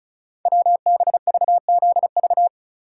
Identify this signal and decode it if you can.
Morse code, heard as a keyed tone, about 36 wpm, 700 Hz. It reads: WBVZV
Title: Code Groups - 5 Letters - 35wpm